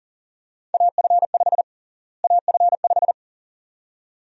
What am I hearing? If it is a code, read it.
Morse code, heard as a keyed tone, about 40 wpm, 700 Hz. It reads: AF5 AF5
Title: USA Call Sign Prefixes - Repeated 2x - 40wpm